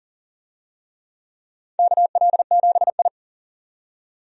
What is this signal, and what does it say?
Morse code, heard as a keyed tone, about 37 wpm, 700 Hz. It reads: KL7I
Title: DXCC-100 Call Signs - 4 Characters - 40wpm